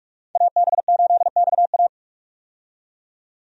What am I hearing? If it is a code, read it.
Morse code, heard as a keyed tone, about 46 wpm, 700 Hz. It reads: AB8XA
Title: USA Call Signs - 5 Characters - 45wpm